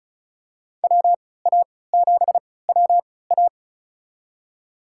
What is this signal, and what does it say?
Morse code, heard as a keyed tone, about 36 wpm, spaced as for 20 wpm, 690 Hz. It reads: WA7WA
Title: USA Call Signs - 5 Characters - 3x Character Spacing - 35wpm